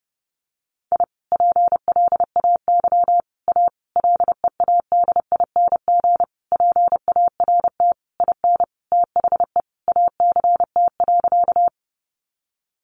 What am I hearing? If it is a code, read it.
Morse code, heard as a keyed tone, about 30 wpm, 700 Hz. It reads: I PLAY A LEADING PART IN THE ACT.
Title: Sentences from Top 1100 Words 30wpm